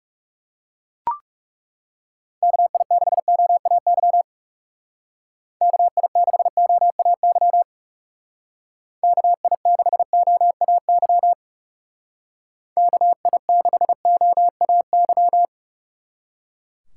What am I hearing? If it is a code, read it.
Morse code, heard as a keyed tone, about 37 wpm, 700 Hz. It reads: KI6OAY KI6OAY KI6OAY KI6OAY
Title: Speed Racing - USA 2x3 Call Signs 30wpm